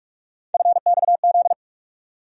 Morse code, heard as a keyed tone, about 41 wpm, 700 Hz. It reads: FXZ
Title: USA Call Sign Postfixes - 3 Letters 45wpm